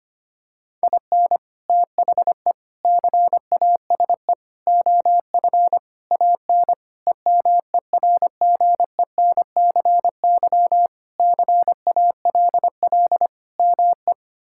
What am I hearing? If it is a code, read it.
Morse code, heard as a keyed tone, about 25 wpm, 700 Hz. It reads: IN THE CASE OF AN EMERGENCY CALL ME
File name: practice / 25wpm / Sentences with 3-Word Phrases - 25wpm